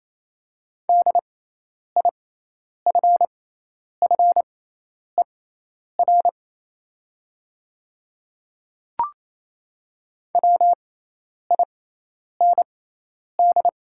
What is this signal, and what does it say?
Morse code, heard as a keyed tone, about 28 wpm, spaced as for 10 wpm, 690 Hz. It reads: DIFFER WIND
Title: 6x Character Spacing - Top 500 Words - 28wpm